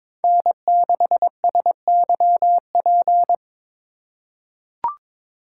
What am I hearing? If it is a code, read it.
Morse code, heard as a keyed tone, about 22 wpm, 700 Hz. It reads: N6SYP
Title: USA Call Signs - 5 Characters - 3x Character Spacing - 22wpm